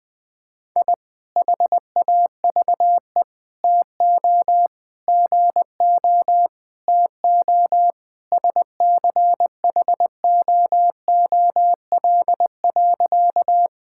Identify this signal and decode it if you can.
Morse code, heard as a keyed tone, about 21 wpm, 700 Hz. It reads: I HAVE TO GO TO SCHOOL.
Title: Sentences with 3-Word Phrases - 20wpm